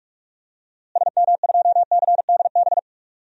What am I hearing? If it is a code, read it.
Morse code, heard as a keyed tone, about 46 wpm, 700 Hz. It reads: SM2CDB
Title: DXCC-100 Call Signs - 6 Characters - Repeated 2x - 45wpm